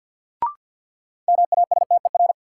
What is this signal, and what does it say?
Morse code, heard as a keyed tone, about 52 wpm, 700 Hz. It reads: MASTER
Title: Sentences from Top 1300 Words - New Word Drill Alt - 50wpm